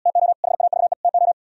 Morse code, heard as a keyed tone, about 77 wpm, 700 Hz. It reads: TO HAVE TO